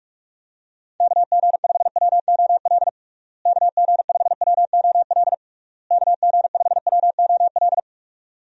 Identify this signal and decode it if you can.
Morse code, heard as a keyed tone, about 47 wpm, 700 Hz. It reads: KG5WOL KG5WOL KG5WOL
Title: USA Call Signs - 6 Characters - Repeated 3x - 45wpm